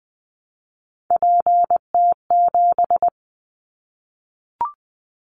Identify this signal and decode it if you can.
Morse code, heard as a keyed tone, about 20 wpm, 700 Hz. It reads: PT7
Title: Callsign Prefixes for Top 50 DXCC - 20wpm